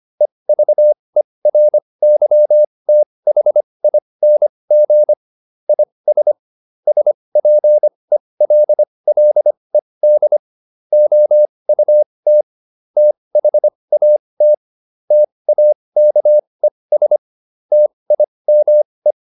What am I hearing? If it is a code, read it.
Morse code, heard as a keyed tone, about 25 wpm, spaced as for 20 wpm, 600 Hz. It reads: EVERYTHING IS SPELLED OUT THAT TAKES TIME